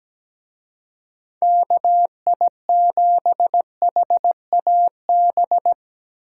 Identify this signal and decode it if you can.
Morse code, heard as a keyed tone, about 20 wpm, 700 Hz. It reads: KI7HAB
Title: Speed Racing - USA 2x3 Call Signs 17wpm